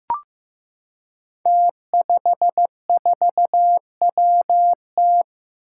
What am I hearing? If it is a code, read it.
Morse code, heard as a keyed tone, about 15 wpm, 700 Hz. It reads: T54WT
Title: Five Challenging Letter-Number Combinations 15wpm